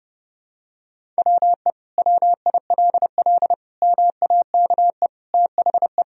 Morse code, heard as a keyed tone, about 30 wpm, 700 Hz. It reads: WE WILL MAKE THE
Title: Sentences from Top 1000 Words - Review Part 1 - 30wpm